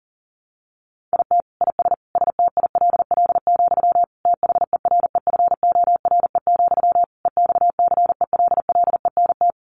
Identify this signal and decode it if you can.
Morse code, heard as a keyed tone, about 40 wpm, 700 Hz. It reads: IT IS STILL, THEREFORE, EXCELLENT
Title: Sentences from Top 1000 Words 40wpm